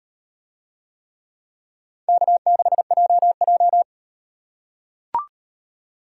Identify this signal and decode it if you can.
Morse code, heard as a keyed tone, about 38 wpm, 700 Hz. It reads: K6JJ